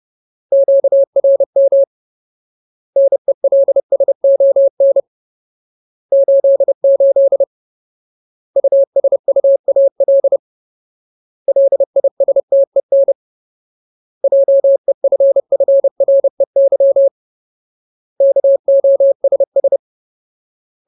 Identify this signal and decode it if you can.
Morse code, heard as a keyed tone, about 30 wpm, 550 Hz. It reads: QRM NELSON 88 USUAL LISTEN JEFFREY KOSS
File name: Ultimate Word List / 30WPM Ultimate